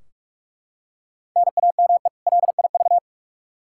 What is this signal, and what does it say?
Morse code, heard as a keyed tone, about 42 wpm, 700 Hz. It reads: NAME LIV
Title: Practice QSO - Exchange 1 - Beginning - 45wpm